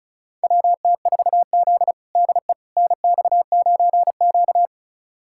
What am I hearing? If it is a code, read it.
Morse code, heard as a keyed tone, about 36 wpm, 700 Hz. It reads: WT4Z DE NX9Q
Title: USA Callsign DE Callsign - 4 Characters - 35wpm